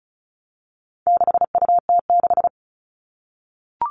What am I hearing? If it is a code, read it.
Morse code, heard as a keyed tone, about 34 wpm, 700 Hz. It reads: 6UT6
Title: All Cut Numbers - 4-Digit 35wpm